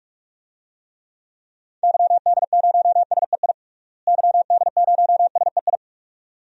Morse code, heard as a keyed tone, about 46 wpm, 700 Hz. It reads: YD0SEI YD0SEI
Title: DXCC-100 Call Signs - 6 Characters - Repeated 2x - 45wpm